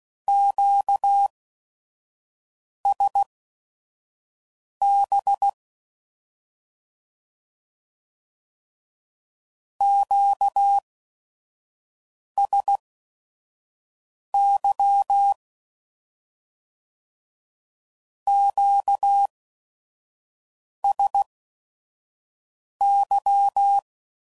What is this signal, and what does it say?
Morse code, heard as a keyed tone, about 16 wpm, spaced as for 5 wpm, 800 Hz. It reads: QSB QSY QSY
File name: CW Code Groups @5WPM / Q Calls / Q Calls Twice